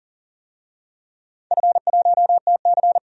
Morse code, heard as a keyed tone, about 42 wpm, 700 Hz. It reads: F1TC
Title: DXCC-100 Call Signs - 4 Characters - 40wpm